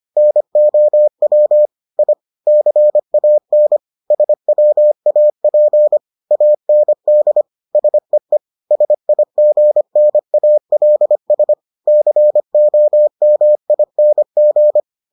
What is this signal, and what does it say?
Morse code, heard as a keyed tone, about 25 wpm, 600 Hz. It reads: NOW I CAN SWAP AND SEE SIGNALS COMING